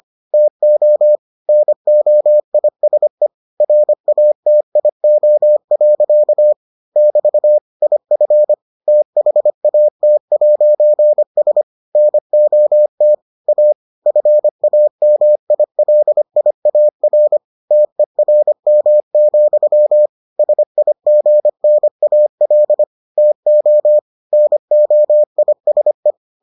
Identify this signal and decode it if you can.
Morse code, heard as a keyed tone, about 25 wpm, 600 Hz. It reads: TO NOISE RATIO. <BT> IF THAT'S NOT A FAMILIAR TERM, SIGNAL TO NOISE